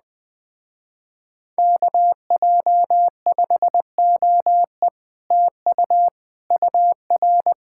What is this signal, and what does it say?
Morse code, heard as a keyed tone, about 20 wpm, 700 Hz. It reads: KJ5OE TU UR
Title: Speed-Racing Common QSO Phrases - 17wpm